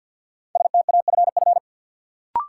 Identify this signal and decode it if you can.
Morse code, heard as a keyed tone, about 46 wpm, 700 Hz. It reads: STAFF
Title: Sets of 5 Words Encore - New Word Drill - 50wpm